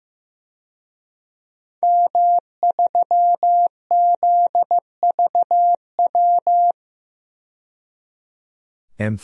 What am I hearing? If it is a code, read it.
Morse code, heard as a keyed tone, about 15 wpm, 700 Hz. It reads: M3ZVW
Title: DXCC-100 Call Signs - 5 Characters - 15wpm